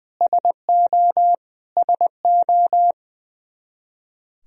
Morse code, heard as a keyed tone, about 20 wpm, 700 Hz. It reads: SO SO
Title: Top 100 Words Repeated 20wpm